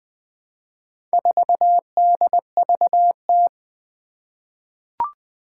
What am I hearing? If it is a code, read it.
Morse code, heard as a keyed tone, about 20 wpm, 700 Hz. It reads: 4DVT
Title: Four Challenging Letter-Number Combinations 20wpm